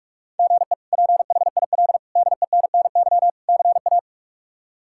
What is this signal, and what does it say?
Morse code, heard as a keyed tone, about 42 wpm, 700 Hz. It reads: GE PHIL DENNY CA
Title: K1USN SST QSO Practice - 45wpm